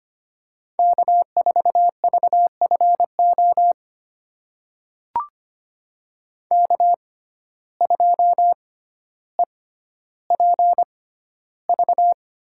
Morse code, heard as a keyed tone, about 25 wpm, spaced as for 13 wpm, 700 Hz. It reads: K4VFO K2EPV
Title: USA Call Signs - 5 Characters - 6x Character Spacing - 25wpm